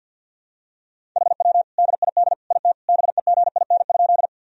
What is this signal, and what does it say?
Morse code, heard as a keyed tone, about 49 wpm, 700 Hz. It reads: HW DID IT BEGIN?
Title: Sentences from Top 300 Words and Abbreviations 50wpm